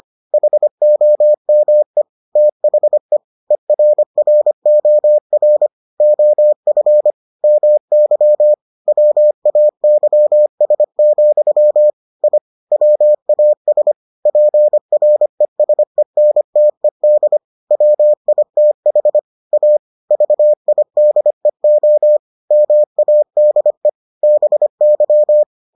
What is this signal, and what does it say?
Morse code, heard as a keyed tone, about 25 wpm, 600 Hz. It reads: HOME THE ERROR OF MY WAYS, I WAS PRESENTED WITH A VIDEO MADE BY